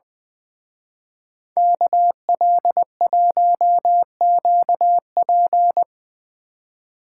Morse code, heard as a keyed tone, about 23 wpm, 700 Hz. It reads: KL1QP
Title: Speed Racing - USA 2x2 Call Signs 20wpm